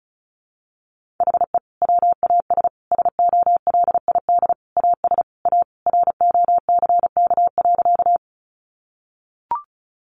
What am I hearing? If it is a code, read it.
Morse code, heard as a keyed tone, about 34 wpm, 700 Hz. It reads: HE WAS SOLID AS A ROCK.
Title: Sentences from Top 1500 Words 35wpm